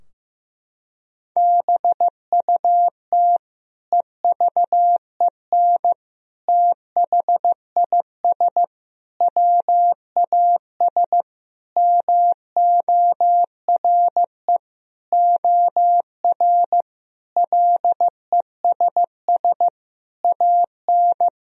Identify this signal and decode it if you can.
Morse code, heard as a keyed tone, about 16 wpm, 700 Hz. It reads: BUT EVEN THIS WAS MORE OR LESS AN
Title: Sentences from Top 1000 Words - Review Part 2 - 15wpm